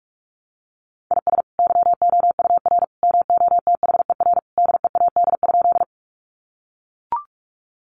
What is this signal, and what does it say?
Morse code, heard as a keyed tone, about 44 wpm, 700 Hz. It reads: IS YOUR MOTHER DEAD?
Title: Sentences from Top 1500 Words 45wpm